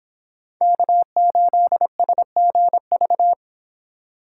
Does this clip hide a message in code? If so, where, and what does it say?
Yes: Morse code from 0.6 to 3.3 s, heard as a keyed tone, about 26 wpm, 700 Hz. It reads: K8SGV